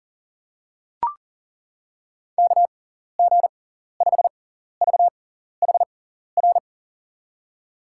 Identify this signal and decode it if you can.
Morse code, heard as a keyed tone, about 40 wpm, spaced as for 14 wpm, 700 Hz. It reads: KG5VHR
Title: USA Call Signs - 6x Character Spacing - 40wpm